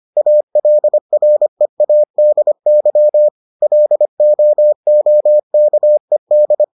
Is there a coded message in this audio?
Yes: Morse code from 0.2 to 6.6 s, heard as a keyed tone, about 25 wpm, 600 Hz. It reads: ALREADY LOOKED